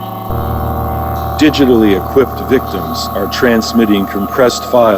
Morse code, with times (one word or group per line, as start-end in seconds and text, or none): none